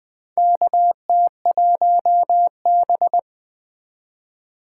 Morse code, heard as a keyed tone, about 20 wpm, 700 Hz. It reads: KT1B
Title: K1USN SST QSO Practice - 20wpm